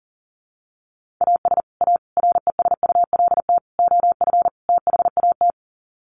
Morse code, heard as a keyed tone, about 40 wpm, 700 Hz. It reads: AS A RESULT OF THAT
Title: Sets of 5 Words Encore 40wpm